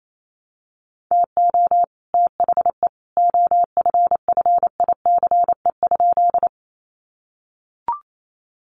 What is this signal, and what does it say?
Morse code, heard as a keyed tone, about 28 wpm, 700 Hz. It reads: TO THE OFFICE?
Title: Sentences from Top 1000 Words 28wpm